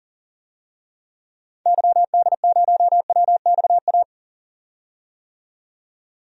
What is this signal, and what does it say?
Morse code, heard as a keyed tone, about 40 wpm, 700 Hz. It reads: YD0WXA
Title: DXCC-100 Call Signs - 6 Characters - 40wpm